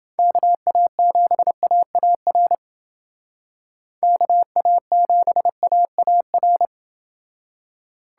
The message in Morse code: KA7AAR KA7AAR

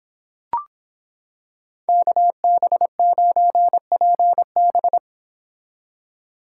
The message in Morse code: KB9PB

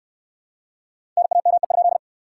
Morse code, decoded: NAME?

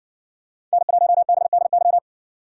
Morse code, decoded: N1BDY

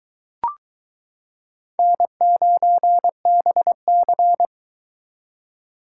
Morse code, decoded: N9BC

